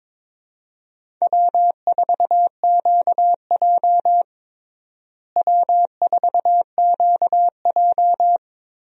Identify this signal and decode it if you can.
Morse code, heard as a keyed tone, about 22 wpm, 700 Hz. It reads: W4QJ W4QJ
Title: USA Call Signs - 4 Characters - Repeated 3x - 22wpm